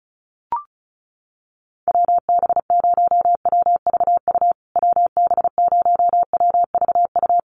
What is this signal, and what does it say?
Morse code, heard as a keyed tone, about 34 wpm, 700 Hz. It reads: WB0WVU WB0WVU
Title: USA Call Signs Repeated 2x 35wpm